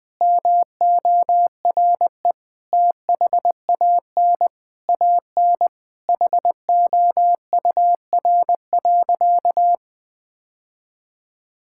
Morse code, heard as a keyed tone, about 20 wpm, 700 Hz. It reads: MORE THAN AN HOUR.
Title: Sentences with 3-Word Phrases - 20wpm